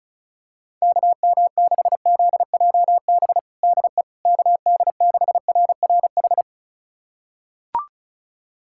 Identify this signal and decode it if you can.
Morse code, heard as a keyed tone, about 36 wpm, 690 Hz. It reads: KM6ZJB DE KD6RRH
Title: USA Callsign DE Callsign 35wpm